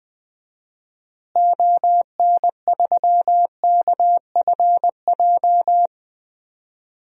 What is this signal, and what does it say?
Morse code, heard as a keyed tone, about 20 wpm, 700 Hz. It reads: ON3KFJ
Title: DXCC-100 Call Signs - 6 Characters - Repeated 3x - 20wpm